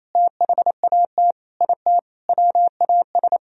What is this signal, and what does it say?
Morse code, heard as a keyed tone, about 28 wpm, 700 Hz. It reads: THAT IT WAS
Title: Sentences from Top 1000 Words - Review Part 4 - 28wpm